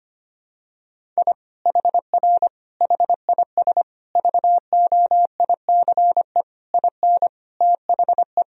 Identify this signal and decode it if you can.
Morse code, heard as a keyed tone, about 25 wpm, 700 Hz. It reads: I HR HIS VOICE IN THE
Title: Sentences from Top 500 Words and Abbreviations 25wpm